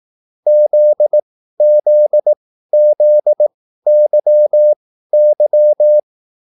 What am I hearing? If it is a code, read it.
Morse code, heard as a keyed tone, about 18 wpm, spaced as for 13 wpm, 600 Hz. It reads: ZZZYY